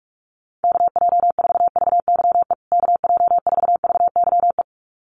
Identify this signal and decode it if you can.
Morse code, heard as a keyed tone, about 44 wpm, 700 Hz. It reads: KJ4VYE KJ4VYE
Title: USA Call Signs Repeated 2x 45wpm